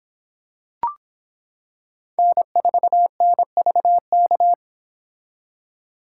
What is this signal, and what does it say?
Morse code, heard as a keyed tone, about 26 wpm, 700 Hz. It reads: N4NVK